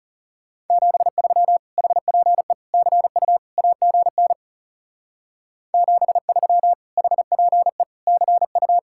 Morse code, heard as a keyed tone, about 38 wpm, 700 Hz. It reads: 73 HPE CU AGN 73 HPE CU